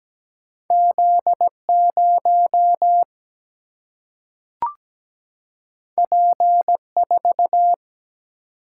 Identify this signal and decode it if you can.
Morse code, heard as a keyed tone, about 17 wpm, 700 Hz. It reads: Z 0 P 4
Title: Single-Letter Single-Number 17wpm